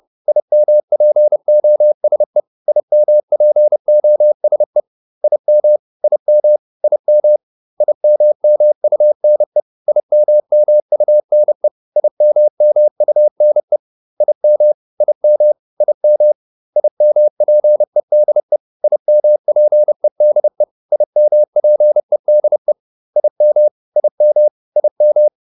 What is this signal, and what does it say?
Morse code, heard as a keyed tone, about 30 wpm, 600 Hz. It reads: IMPOSE IMPOSE IM IM IM IMMUNE IMMUNE IMMUNE IM IM IM IMPEDE IMPEDE IMPEDE IM IM IM
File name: IM6 30